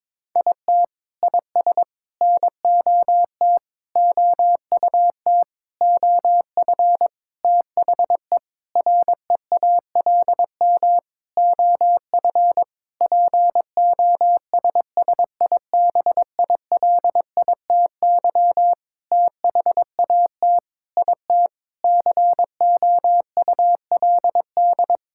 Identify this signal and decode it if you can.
Morse code, heard as a keyed tone, about 22 wpm, 700 Hz. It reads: IT IS NOT OUT OF THE REALM OF POSSIBILITY THAT IT COULD
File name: Sentences with 9-Word Phrases - 22wpm